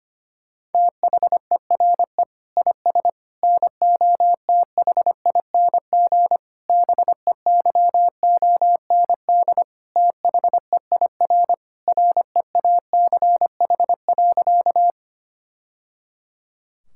Morse code, heard as a keyed tone, about 25 wpm, 700 Hz. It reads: THERE IS NOTHING BEYOND THEIR REACH.
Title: Sentences from Top 1000 Words - Review Part 6 - 25wpm